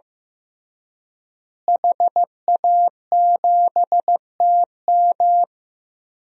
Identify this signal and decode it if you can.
Morse code, heard as a keyed tone, about 16 wpm, 700 Hz. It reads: HA7TM
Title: DXCC-100 Call Signs - 5 Characters - Repeated 2x - 15wpm